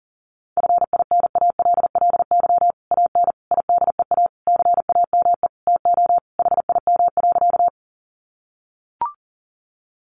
Morse code, heard as a keyed tone, about 40 wpm, 700 Hz. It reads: FINALLY AN IDEA CAME TO HIM.
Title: Sentences from Top 1000 Words 40wpm